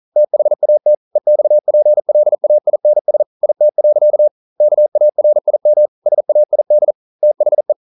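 Morse code, heard as a keyed tone, about 42 wpm, 600 Hz. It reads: THAT EXPLAINS IT. KARIM SAID THE
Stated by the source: Morse code at 40 wpm, practice text, CW med Gnister.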